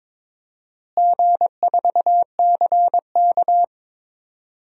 Morse code, heard as a keyed tone, about 22 wpm, 700 Hz. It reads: G4CK
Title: DXCC-100 Call Signs - 4 Characters - Repeated 3x - 22wpm